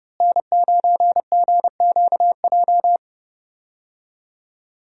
Morse code, heard as a keyed tone, about 32 wpm, 700 Hz. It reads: N9GQJ
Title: USA Call Signs - 5 Characters - Repeated 3x - 30wpm